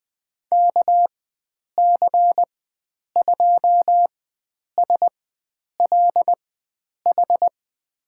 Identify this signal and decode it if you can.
Morse code, heard as a keyed tone, about 20 wpm, spaced as for 9 wpm, 700 Hz. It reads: KC2SLH
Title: USA Call Signs - 6 Characters - 4x Character Spacing - 20wpm